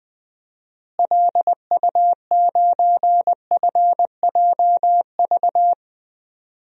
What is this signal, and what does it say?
Morse code, heard as a keyed tone, about 20 wpm, 700 Hz. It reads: LU9FJV
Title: DXCC-100 Call Signs - 6 Characters - Repeated 3x - 20wpm